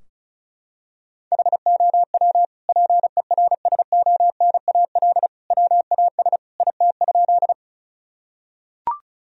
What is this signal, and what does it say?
Morse code, heard as a keyed tone, about 34 wpm, 700 Hz. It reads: HOW PERSONAL WAS IT?
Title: Sentences from Top 1000 Words - Review Part 4 - 35wpm